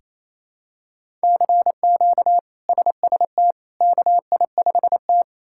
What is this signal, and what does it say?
Morse code, heard as a keyed tone, about 28 wpm, 700 Hz. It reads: CQ SST KI5T
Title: K1USN SST QSO Practice - 28wpm